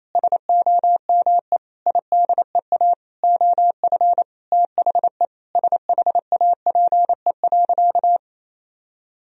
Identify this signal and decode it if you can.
Morse code, heard as a keyed tone, about 28 wpm, 700 Hz. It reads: SOME IDEA OF THE SHAPE.
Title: Sentences from Top 1000 Words - Review Part 3 - 28wpm